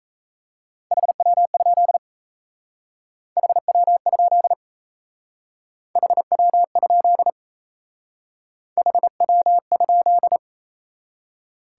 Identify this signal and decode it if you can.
Morse code, heard as a keyed tone, about 34 wpm, 700 Hz. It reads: HW? HW? HW? HW?
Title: Speed-Racing Top 180 QSO Elements 28wpm